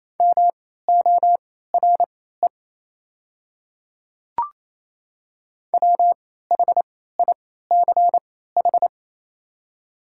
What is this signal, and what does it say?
Morse code, heard as a keyed tone, about 28 wpm, spaced as for 16 wpm, 700 Hz. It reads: MORE WHICH
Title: 3x Character Spacing - Top 100 Words - 28wpm